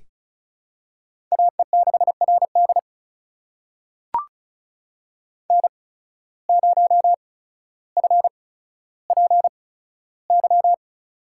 Morse code, heard as a keyed tone, about 36 wpm, spaced as for 15 wpm, 700 Hz. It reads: AE6RD N0FPY
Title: USA Call Signs - 5 Characters - 8x Character Spacing - 35wpm